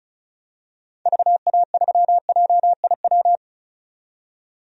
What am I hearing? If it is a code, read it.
Morse code, heard as a keyed tone, about 36 wpm, 700 Hz. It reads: VA3JIW
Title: DXCC-100 Call Signs - 6 Characters - 35wpm